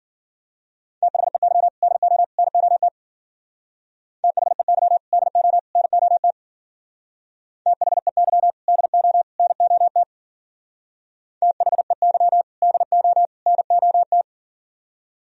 Morse code, heard as a keyed tone, about 50 wpm, 700 Hz. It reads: THEY DO NOT THEY DO NOT THEY DO NOT THEY DO NOT
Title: Speed-Racing - Sets of 3 Words 40wpm